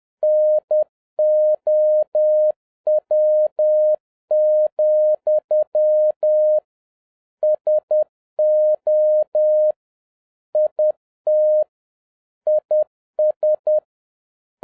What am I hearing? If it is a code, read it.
Morse code, heard as a keyed tone, about 10 wpm, 610 Hz. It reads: NOW, SO IT IS